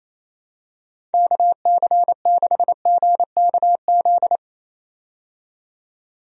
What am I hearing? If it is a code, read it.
Morse code, heard as a keyed tone, about 28 wpm, 700 Hz. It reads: KC6GKZ
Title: USA Call Signs - 6 Characters - 28wpm